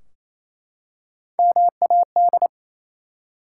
Morse code, heard as a keyed tone, about 27 wpm, 700 Hz. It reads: MAD
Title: Three Letter Words - 28wpm